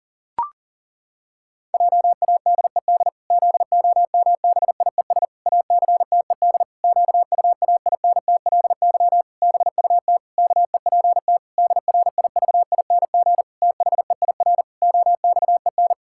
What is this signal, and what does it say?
Morse code, heard as a keyed tone, about 37 wpm, 700 Hz. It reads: JADED ZOMBIES ACTED QUAINTLY BUT KEPT DRIVING THEIR OXEN
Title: Pangram Challenge v2 - 40wpm